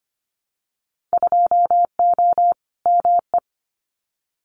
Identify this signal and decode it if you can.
Morse code, heard as a keyed tone, about 24 wpm, 700 Hz. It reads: 2O ME
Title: Winter Field Day Exchanges Encore 25wpm